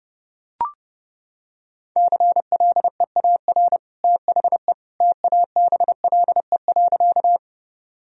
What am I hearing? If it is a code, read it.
Morse code, heard as a keyed tone, about 32 wpm, 700 Hz. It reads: CLEAR THE TABLE.
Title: Sentences from Top 1000 Words - Review Part 2 - 30wpm